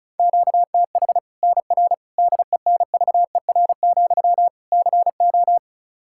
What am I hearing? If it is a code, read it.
Morse code, heard as a keyed tone, about 34 wpm, 700 Hz. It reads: QTH NR DENVER, CO
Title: K1USN SST QSO Practice 35wpm